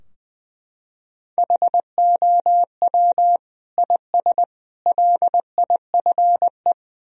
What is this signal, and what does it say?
Morse code, heard as a keyed tone, about 20 wpm, 700 Hz. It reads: HOW IS LIFE
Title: Sentences from Top 1000 Words - Review Part 2 - 20wpm